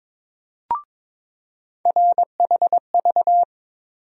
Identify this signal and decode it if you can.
Morse code, heard as a keyed tone, about 22 wpm, 700 Hz. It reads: RHV
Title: USA Call Sign Postfixes - 3 Letters 22wpm